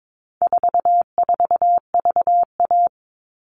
Morse code, heard as a keyed tone, about 22 wpm, 700 Hz. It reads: 44VA
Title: All Cut Numbers - 4-Digit 22wpm